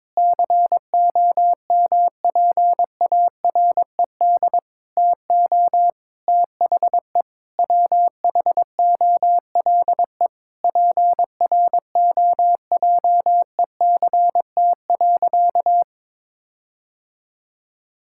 Morse code, heard as a keyed tone, about 21 wpm, 700 Hz. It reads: COMPARED TO THE WHOLE PROJECT.
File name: Sentences with 7-Word Phrases - 22wpm